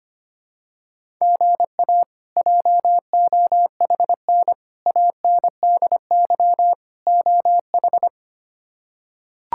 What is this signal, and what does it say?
Morse code, heard as a keyed tone, about 25 wpm, 700 Hz. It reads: GA JOHN ANDY OH
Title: K1USN SST QSO Practice - 25wpm